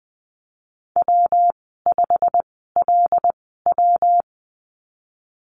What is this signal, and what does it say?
Morse code, heard as a keyed tone, about 20 wpm, spaced as for 15 wpm, 700 Hz. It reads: W5LW
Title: 167 Lesson 27g - Number 7 - Call Signs 20wpm